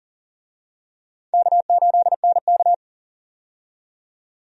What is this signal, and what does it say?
Morse code, heard as a keyed tone, about 40 wpm, 700 Hz. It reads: K8NK
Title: USA Call Signs - 4 Characters - 40wpm